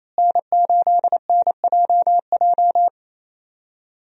Morse code, heard as a keyed tone, about 28 wpm, 700 Hz. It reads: N8NJJ
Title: USA Call Signs - 5 Characters - Repeated 2x - 28wpm